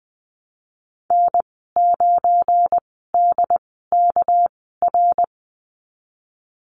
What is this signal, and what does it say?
Morse code, heard as a keyed tone, about 20 wpm, spaced as for 15 wpm, 700 Hz. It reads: N9DKR
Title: 200 Lesson 32g - Letter k - Call Signs 20wpm